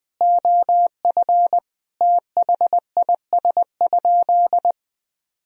Morse code, heard as a keyed tone, about 21 wpm, 700 Hz. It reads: OF THIS?
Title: Sentences from Top 1000 Words - Review Part 1 - 20wpm